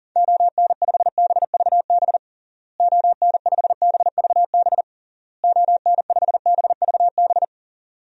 Morse code, heard as a keyed tone, about 40 wpm, 700 Hz. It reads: ON5BVB ON5BVB ON5BVB
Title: DXCC-100 Call Signs - 6 Characters - Repeated 3x - 40wpm